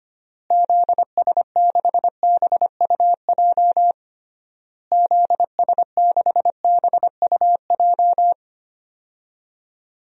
Morse code, heard as a keyed tone, about 25 wpm, 700 Hz. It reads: ZS6BUJ ZS6BUJ
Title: DXCC-100 Call Signs - 6 Characters - Repeated 3x - 25wpm